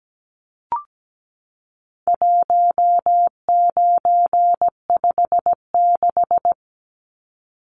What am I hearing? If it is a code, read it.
Morse code, heard as a keyed tone, about 17 wpm, 700 Hz. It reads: 1956
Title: Numbers - 4-Digit 17wpm